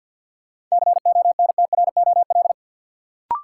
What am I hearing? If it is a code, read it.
Morse code, heard as a keyed tone, about 53 wpm, 700 Hz. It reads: CONTROL
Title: Sentences from Top 1000 Words - Review Part 2 - New Word Drill - 50wpm